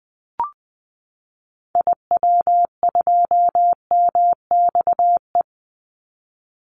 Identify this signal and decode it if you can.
Morse code, heard as a keyed tone, about 20 wpm, 700 Hz. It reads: IW2MXE
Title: Call Signs Repeated 2x - 20wpm